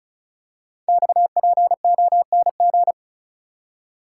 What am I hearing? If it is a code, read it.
Morse code, heard as a keyed tone, about 36 wpm, 690 Hz. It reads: XPONG
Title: Code Groups - 5 Letters - 35wpm